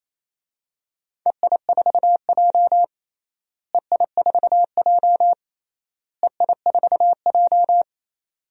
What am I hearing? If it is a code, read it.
Morse code, heard as a keyed tone, about 28 wpm, 700 Hz. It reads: EI4J EI4J EI4J
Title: DXCC-100 Call Signs - 4 Characters - Repeated 3x - 28wpm